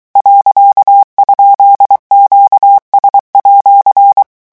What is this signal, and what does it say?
Morse code, heard as a keyed tone, about 23 wpm, 800 Hz. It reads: .?QS@
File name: morse mp3 / revision19-vitesse 22 mots